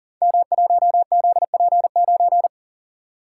Morse code, heard as a keyed tone, about 38 wpm, 700 Hz. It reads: M1ZP9
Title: Code Groups - 5 Letter:Numbers - 40wpm